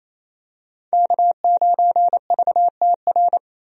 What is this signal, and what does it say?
Morse code, heard as a keyed tone, about 28 wpm, 700 Hz. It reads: K9VTR
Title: USA Call Signs - 5 Characters - 28wpm